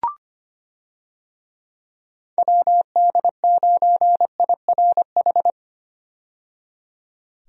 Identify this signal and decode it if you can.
Morse code, heard as a keyed tone, about 25 wpm, 700 Hz. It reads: WD9IRH
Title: USA Call Signs - 6 Characters - 25wpm